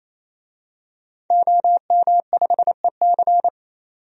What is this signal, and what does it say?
Morse code, heard as a keyed tone, about 28 wpm, 700 Hz. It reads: OM5EC
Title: DXCC-100 Call Signs - 5 Characters - 28wpm